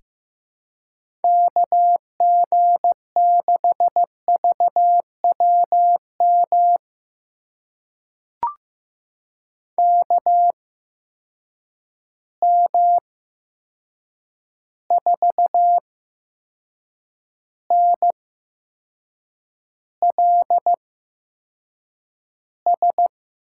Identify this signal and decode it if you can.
Morse code, heard as a keyed tone, about 15 wpm, spaced as for 6 wpm, 700 Hz. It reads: KG6VWM KM4NLS
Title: USA Call Signs - 6 Characters - 8x Character Spacing - 15wpm